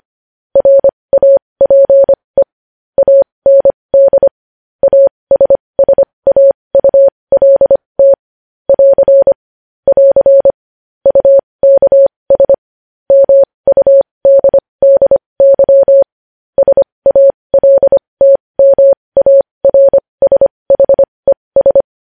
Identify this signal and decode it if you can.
Morse code, heard as a keyed tone, about 25 wpm, spaced as for 20 wpm, 550 Hz. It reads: RAPE AND ASSAULT <AR> <AR> UKS MUDDY SALTMARSHES
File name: podcast / News headlines-1x-25x20